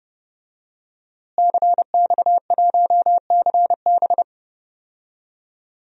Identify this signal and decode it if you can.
Morse code, heard as a keyed tone, about 30 wpm, 700 Hz. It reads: CX1CB
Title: DXCC-100 Call Signs - 5 Characters - Repeated 3x - 30wpm